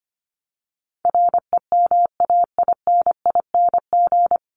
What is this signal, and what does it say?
Morse code, heard as a keyed tone, about 25 wpm, 700 Hz. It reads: REMAINING
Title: ING Suffix - Top 100 Words with Base Words 25wpm